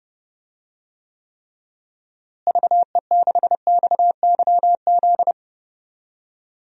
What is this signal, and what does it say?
Morse code, heard as a keyed tone, about 30 wpm, 700 Hz. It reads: VE6XYZ
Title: DXCC-100 Call Signs - 6 Characters - Repeated 3x - 30wpm